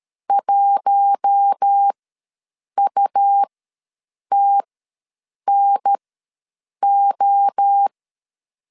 Morse code, heard as a keyed tone, about 13 wpm, spaced as for 7 wpm, 800 Hz. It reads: 1UTNO
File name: 5 LB ws 12wpm